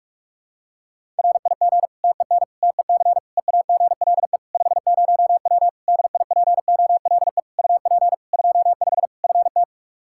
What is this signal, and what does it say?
Morse code, heard as a keyed tone, about 41 wpm, 700 Hz. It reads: RIG TEN TEC EAGLE 50W DIPOLE UP 25 FT
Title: Practice QSO - Exchange 2 - Info - 45wpm